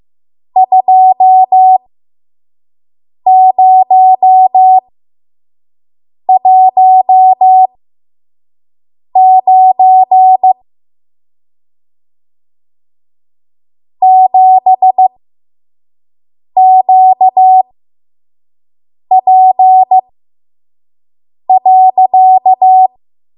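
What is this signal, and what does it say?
Morse code, heard as a keyed tone, about 15 wpm, spaced as for 5 wpm, 750 Hz. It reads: 2019 7QP.
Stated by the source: 5 WPM Morse code practice.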